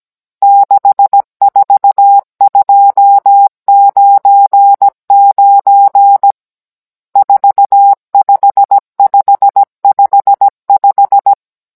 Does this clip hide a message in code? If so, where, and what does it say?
Yes: Morse code from 0.4 to 11.3 s, heard as a keyed tone, about 17 wpm, 800 Hz. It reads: 64299 45555